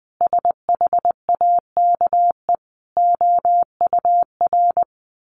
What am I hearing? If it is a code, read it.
Morse code, heard as a keyed tone, about 20 wpm, 700 Hz. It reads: SHAKE OUR
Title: A Sound Of Thunder 20wpm